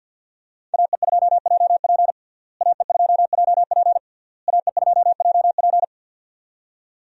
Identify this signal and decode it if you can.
Morse code, heard as a keyed tone, about 49 wpm, 700 Hz. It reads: AE2JP AE2JP AE2JP
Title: USA Call Signs - 5 Characters - Repeated 3x - 50wpm